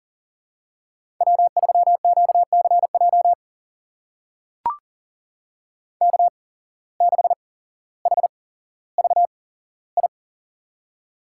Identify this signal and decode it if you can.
Morse code, heard as a keyed tone, about 40 wpm, spaced as for 17 wpm, 700 Hz. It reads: W3QCJ K6HVI
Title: USA Call Signs - 5 Characters - 8x Character Spacing - 40wpm